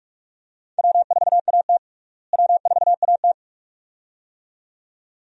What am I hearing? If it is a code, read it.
Morse code, heard as a keyed tone, about 45 wpm, 700 Hz. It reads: W4AT W4AT
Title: USA Call Signs - 4 Characters - Repeated 2x - 45wpm